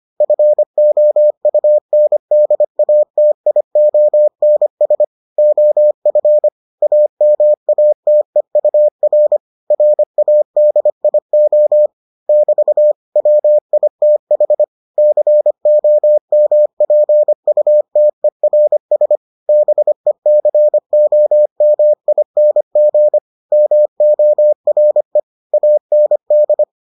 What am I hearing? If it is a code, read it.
Morse code, heard as a keyed tone, about 25 wpm, 600 Hz. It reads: FOUNDATIONS OF AMATEUR RADIO <BT> WITH COMPUTERS BECOMING MORE AND